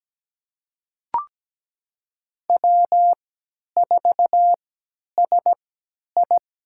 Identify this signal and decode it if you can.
Morse code, heard as a keyed tone, about 17 wpm, spaced as for 10 wpm, 700 Hz. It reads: W4SI
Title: USA Call Signs - 4 Characters - 3x Character Spacing - 17wpm